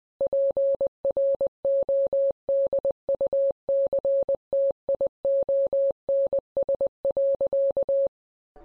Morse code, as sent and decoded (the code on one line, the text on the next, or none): .--. .-. --- -.. ..- -.-. - .. --- -. ... .-.-.-
PRODUCTIONS.